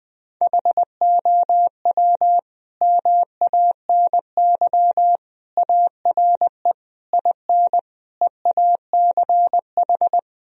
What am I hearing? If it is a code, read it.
Morse code, heard as a keyed tone, about 20 wpm, 700 Hz. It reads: HOW MANY ARE IN EACH
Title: Sentences from Top 1000 Words - Review Part 2 - 20wpm